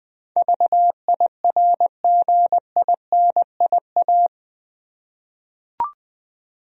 Morse code, heard as a keyed tone, about 20 wpm, 700 Hz. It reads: VIRGINIA
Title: USA Common Names - 20wpm